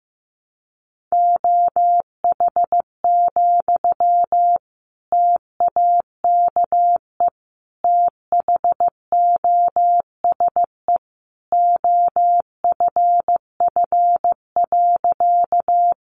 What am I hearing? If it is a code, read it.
Morse code, heard as a keyed tone, about 15 wpm, 700 Hz. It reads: OH, TAKE THOSE OFF.
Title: Sentences from Top 500 Words 15wpm